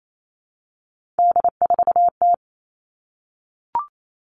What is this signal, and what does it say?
Morse code, heard as a keyed tone, about 28 wpm, 700 Hz. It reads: D4T
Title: All Cut Numbers - 3-Digit 28wpm